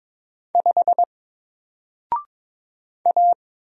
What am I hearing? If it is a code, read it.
Morse code, heard as a keyed tone, about 22 wpm, 700 Hz. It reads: 5 A
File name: Single-Letter-Number-22wpm